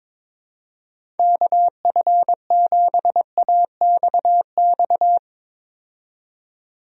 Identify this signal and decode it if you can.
Morse code, heard as a keyed tone, about 21 wpm, 700 Hz. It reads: KF7AXX
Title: USA Call Signs - 6 Characters - 22wpm